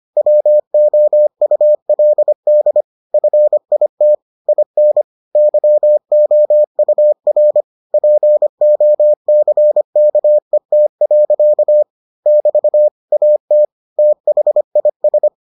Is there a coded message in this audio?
Yes: Morse code from 0.2 to 15.3 s, heard as a keyed tone, about 25 wpm, 600 Hz. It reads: WOULD FIT IN YOUR POCKET. <BT> AT THIS